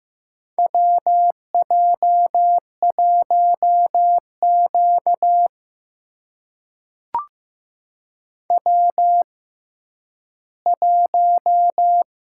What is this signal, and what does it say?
Morse code, heard as a keyed tone, about 15 wpm, spaced as for 10 wpm, 700 Hz. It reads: WJ1Q W1
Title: USA Call Signs - 4 Characters - 6x Character Spacing - 15wpm